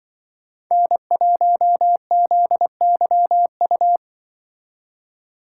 Morse code, heard as a keyed tone, about 25 wpm, 700 Hz. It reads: N1ZYU
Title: Speed Racing - USA 1x3 Call Signs 20wpm